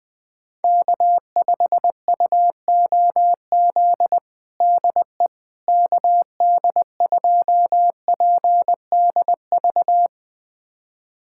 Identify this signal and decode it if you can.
Morse code, heard as a keyed tone, about 20 wpm, 700 Hz. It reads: K5UOZ DE KD2PDV
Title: USA Callsign DE Callsign 20wpm